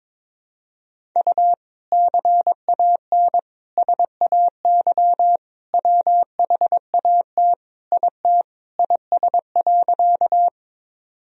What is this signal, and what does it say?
Morse code, heard as a keyed tone, about 22 wpm, 700 Hz. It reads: U CAN SAY WHAT IT IS.
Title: Sentences from Top 200 Words and Abbreviations 22wpm